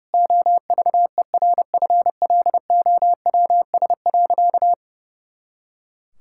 Morse code, heard as a keyed tone, about 30 wpm, 700 Hz. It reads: OVERFLOWS.
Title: Pangram Challenge v2 - 30wpm